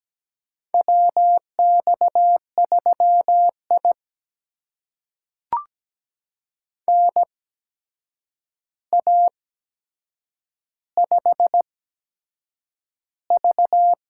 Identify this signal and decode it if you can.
Morse code, heard as a keyed tone, about 17 wpm, spaced as for 7 wpm, 700 Hz. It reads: WX3I NA5V
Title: USA Call Signs - 4 Characters - 8x Character Spacing - 17wpm